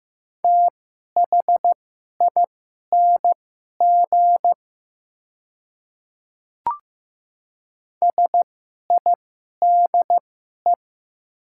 Morse code, heard as a keyed tone, about 15 wpm, spaced as for 11 wpm, 690 Hz. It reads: THING SIDE
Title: 2x Character Spacing - Top 100 Words - 15wpm